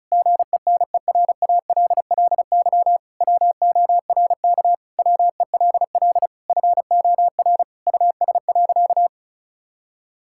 Morse code, heard as a keyed tone, about 36 wpm, 700 Hz. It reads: GENERALLY WORK WELL FOR US.
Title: Sentences from Top 1000 Words - Review Part 6 - 35wpm